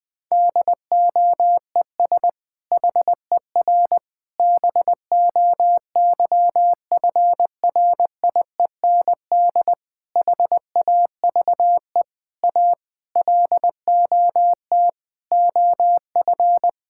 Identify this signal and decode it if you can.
Morse code, heard as a keyed tone, about 21 wpm, 700 Hz. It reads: DOES HER BOYFRIEND HAVE A LOT OF
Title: Sentences from Top 1000 Words - Review Part 5 - 20wpm